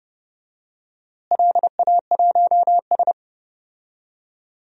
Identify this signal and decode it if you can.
Morse code, heard as a keyed tone, about 30 wpm, 700 Hz. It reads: LA1S
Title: DXCC-100 Call Signs - 4 Characters - Repeated 2x - 30wpm